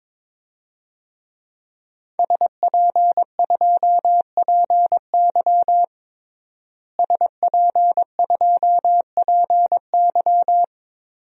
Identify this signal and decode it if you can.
Morse code, heard as a keyed tone, about 22 wpm, 700 Hz. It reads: SP2PY SP2PY
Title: DXCC-100 Call Signs - 5 Characters - Repeated 2x - 22wpm